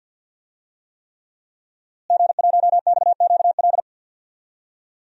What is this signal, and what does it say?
Morse code, heard as a keyed tone, about 53 wpm, 700 Hz. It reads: G1XQL
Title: DXCC-100 Call Signs - 5 Characters - Repeated 3x - 50wpm